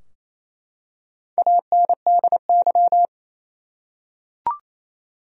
Morse code, heard as a keyed tone, about 27 wpm, 700 Hz. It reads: ANDY
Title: USA Common Names - Max 4 Letters - 28wpm